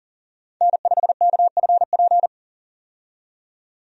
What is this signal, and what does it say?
Morse code, heard as a keyed tone, about 40 wpm, 700 Hz. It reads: N5KFP
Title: USA Call Signs - 5 Characters - 40wpm